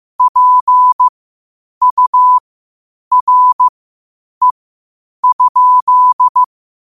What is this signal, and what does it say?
Morse code, heard as a keyed tone, about 15 wpm, spaced as for 8 wpm, 1000 Hz. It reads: PURE?
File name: qotdfiles / 8wpm